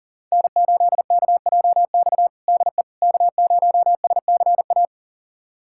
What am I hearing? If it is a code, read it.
Morse code, heard as a keyed tone, about 37 wpm, 700 Hz. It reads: N8KJX DE K0SCA
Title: USA Callsign DE Callsign - 5 Characters - 40wpm